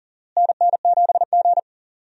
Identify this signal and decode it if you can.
Morse code, heard as a keyed tone, about 40 wpm, 700 Hz. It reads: NN7G
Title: USA Call Signs - 4 Characters - Repeated 2x - 40wpm